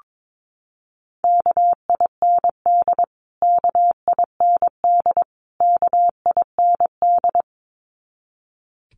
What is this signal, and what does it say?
Morse code, heard as a keyed tone, about 22 wpm, 700 Hz. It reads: KIND KIND KIND
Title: Top 200 Words Repeated 22wpm